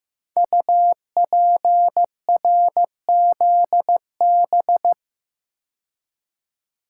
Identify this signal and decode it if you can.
Morse code, heard as a keyed tone, about 15 wpm, 700 Hz. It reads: UPRZB
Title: Code Groups - 5 Letters - 15wpm